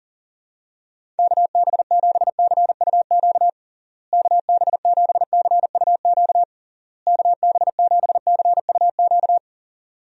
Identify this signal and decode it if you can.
Morse code, heard as a keyed tone, about 40 wpm, 700 Hz. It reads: KB7CUQ KB7CUQ KB7CUQ
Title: USA Call Signs - 6 Characters - Repeated 3x - 40wpm